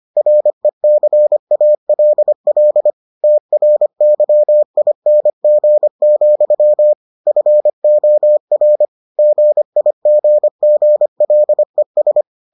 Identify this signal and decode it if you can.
Morse code, heard as a keyed tone, about 25 wpm, 600 Hz. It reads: RECALL TRYING, FOR GIGGLES